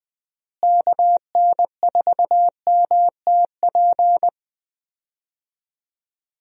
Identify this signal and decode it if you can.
Morse code, heard as a keyed tone, about 20 wpm, 700 Hz. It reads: KN4MTP
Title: USA Call Signs - 6 Characters - Repeated 3x - 20wpm